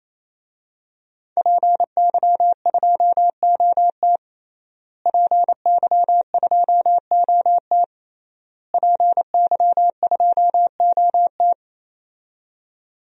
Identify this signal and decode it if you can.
Morse code, heard as a keyed tone, about 27 wpm, 700 Hz. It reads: PY2OT PY2OT PY2OT
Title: DXCC-100 Call Signs - 5 Characters - Repeated 3x - 28wpm